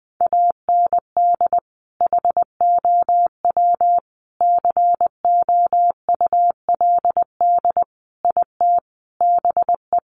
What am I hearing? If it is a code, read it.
Morse code, heard as a keyed tone, about 20 wpm, 700 Hz. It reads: AND HOW COULD IT BE
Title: Sentences from Top 1200 Words 20wpm